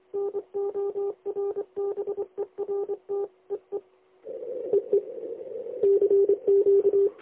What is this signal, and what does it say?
Morse code, heard as a keyed tone, about 25 wpm, 400 Hz. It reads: NORBERT EE EE CQ
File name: lx1nj- -1005utc-10110khz2-49235